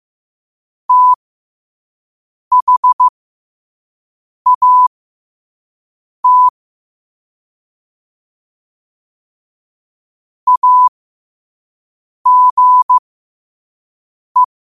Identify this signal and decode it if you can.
Morse code, heard as a keyed tone, about 15 wpm, spaced as for 5 wpm, 1000 Hz. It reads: THAT AGE